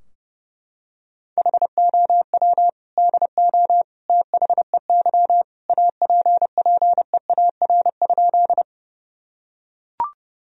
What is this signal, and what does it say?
Morse code, heard as a keyed tone, about 30 wpm, 700 Hz. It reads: HOW DO THEY APPEAR?
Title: Sentences from Top 1000 Words - Review Part 2 - 30wpm